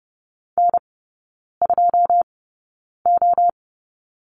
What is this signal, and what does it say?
Morse code, heard as a keyed tone, about 32 wpm, spaced as for 9 wpm, 700 Hz. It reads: N2O
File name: 081 Lesson 14e - Number 2 - Three Characters 30wpm